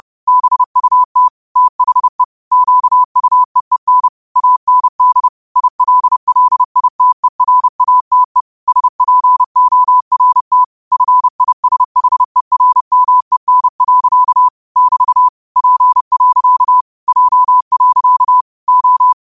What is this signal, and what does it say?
Morse code, heard as a keyed tone, about 30 wpm, 1000 Hz. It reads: BUT THE QUEEN AND ILLITERATE SPORT FISHERMEN. <BT> P. J. O
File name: qotdfiles / 30wpm